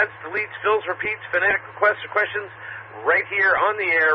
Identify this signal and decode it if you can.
Emergency Alert System (SAME) tones, heard as harsh buzzing bursts, with speech. It is heard from 1.0 to 3.6 s.